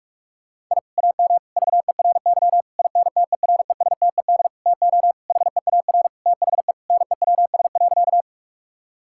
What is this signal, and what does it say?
Morse code, heard as a keyed tone, about 41 wpm, 700 Hz. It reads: I AM VERY INTERESTED TO HEAR THE NEWS.
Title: Sentences from Top 1000 Words - Review Part 6 - 45wpm